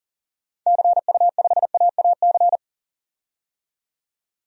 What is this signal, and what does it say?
Morse code, heard as a keyed tone, about 40 wpm, 700 Hz. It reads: CU5AAC
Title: DXCC-100 Call Signs - 6 Characters - 40wpm